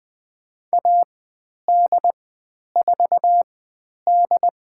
Morse code, heard as a keyed tone, about 20 wpm, spaced as for 10 wpm, 700 Hz. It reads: AD4D
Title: 102 Lesson 17g - Letter u - Call Signs 20wpm